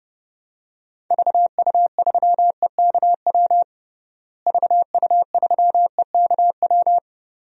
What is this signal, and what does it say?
Morse code, heard as a keyed tone, about 30 wpm, 700 Hz. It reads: VU3EKW VU3EKW
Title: DXCC-100 Call Signs - 6 Characters - Repeated 3x - 30wpm